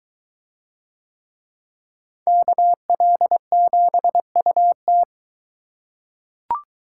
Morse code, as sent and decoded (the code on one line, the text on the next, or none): -.- .-.. --... ..- -
KL7UT